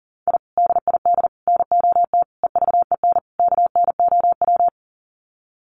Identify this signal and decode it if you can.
Morse code, heard as a keyed tone, about 40 wpm, 700 Hz. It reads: I DID NOT EVEN KNOW
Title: Sets of 5 Words Encore 40wpm